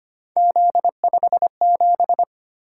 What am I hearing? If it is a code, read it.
Morse code, heard as a keyed tone, about 25 wpm, 700 Hz. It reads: Z57
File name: Three Letter-Number Combinations - 25wpm